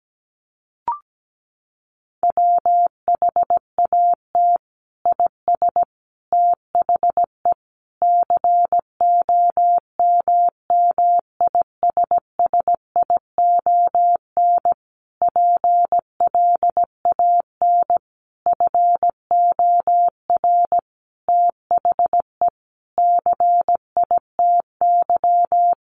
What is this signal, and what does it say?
Morse code, heard as a keyed tone, about 17 wpm, 700 Hz. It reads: WHAT IS THE COMMISSION PLAN FOR THE CITY
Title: Sentences from Top 1400 Words 17wpm